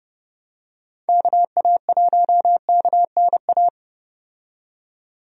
Morse code, heard as a keyed tone, about 30 wpm, 700 Hz. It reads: KA1KNA
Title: USA Call Signs - 6 Characters - 30wpm